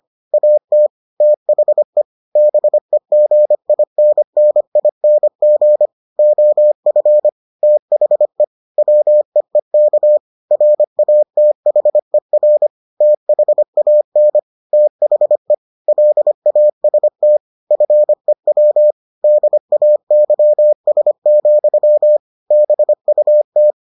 Morse code, heard as a keyed tone, about 25 wpm, 600 Hz. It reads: AT THE BEGINNING OF THE WEEK RATHER THAN THE LAST FEW DAYS, BUT